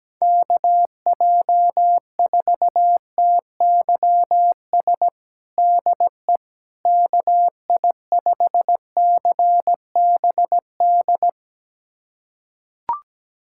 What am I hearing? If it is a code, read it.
Morse code, heard as a keyed tone, about 17 wpm, 700 Hz. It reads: KJ4TYS DE KI5CBD
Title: USA Callsign DE Callsign - 6 Characters - 17wpm